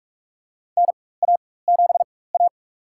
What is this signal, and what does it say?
Morse code, heard as a keyed tone, about 43 wpm, spaced as for 21 wpm, 700 Hz. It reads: NA7A
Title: USA Call Signs - 4 Characters - 4x Character Spacing - 45wpm